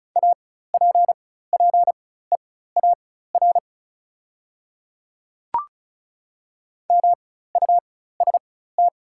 Morse code, heard as a keyed tone, about 36 wpm, spaced as for 16 wpm, 700 Hz. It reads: APPEAR MUST
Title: 4x Character Spacing - Top 500 Words - 35wpm